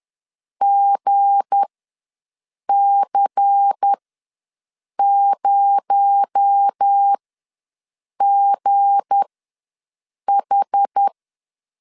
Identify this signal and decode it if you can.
Morse code, heard as a keyed tone, about 10 wpm, spaced as for 6 wpm, 800 Hz. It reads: GC0GH
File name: LEZIONI / 7 GF9Z / 7 G ws 10wpm 3843